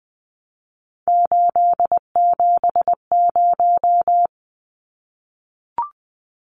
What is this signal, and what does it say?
Morse code, heard as a keyed tone, about 20 wpm, 700 Hz. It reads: 870
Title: Numbers - 3-Digit 20wpm